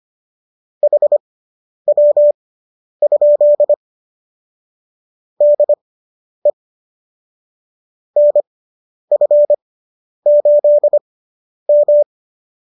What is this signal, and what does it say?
Morse code, heard as a keyed tone, about 25 wpm, spaced as for 10 wpm, 590 Hz. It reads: HW? DE NF8M